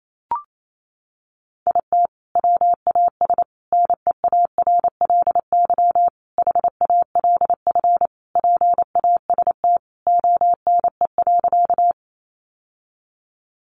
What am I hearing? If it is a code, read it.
Morse code, heard as a keyed tone, about 28 wpm, 700 Hz. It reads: IT WAS NEARLY HALF PAST ONE.
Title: Sentences from Top 1300 Words 28wpm